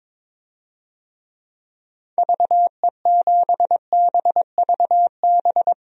Morse code, heard as a keyed tone, about 22 wpm, 700 Hz. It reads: VE7BVB
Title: DXCC-100 Call Signs - 6 Characters - 22wpm